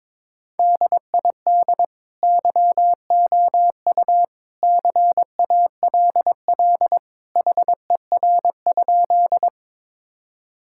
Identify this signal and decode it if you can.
Morse code, heard as a keyed tone, about 22 wpm, 700 Hz. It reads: DID YOU CALL HER?
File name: practice / 22wpm / Sentences with 2-Word Phrases - 22wpm